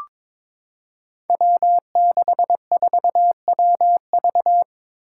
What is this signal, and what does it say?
Morse code, heard as a keyed tone, about 22 wpm, 700 Hz. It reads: W64WV
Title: Five Challenging Letter-Number Combinations 22wpm